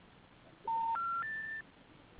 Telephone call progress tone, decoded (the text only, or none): special information tone